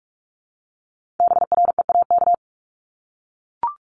Morse code, heard as a keyed tone, about 46 wpm, 700 Hz. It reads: BREAK
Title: Multi-letter Phonemes Repeated 3x 45wpm